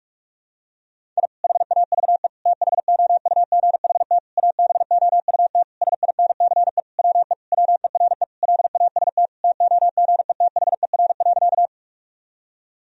Morse code, heard as a keyed tone, about 41 wpm, 700 Hz. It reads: I HAVE THOUGHT ABOUT SINCE WE WERE LAST TOGETHER.
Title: Sentences from Top 1000 Words - Review Part 1 - 45wpm